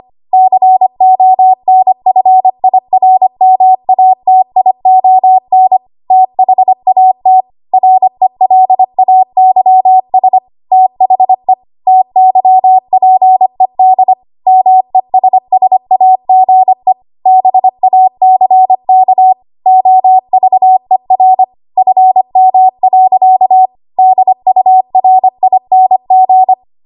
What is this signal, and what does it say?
Morse code, heard as a keyed tone, about 25 wpm, 750 Hz. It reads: CONFIRMATION THAT RELAYS THE TYPED MESSAGE BACK OVER FM. DURING
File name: Morse / 25WPM